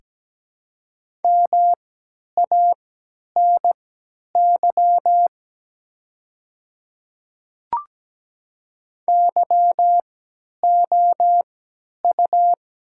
Morse code, heard as a keyed tone, about 17 wpm, spaced as for 10 wpm, 700 Hz. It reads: MANY YOU
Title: 3x Character Spacing - Top 100 Words - 17wpm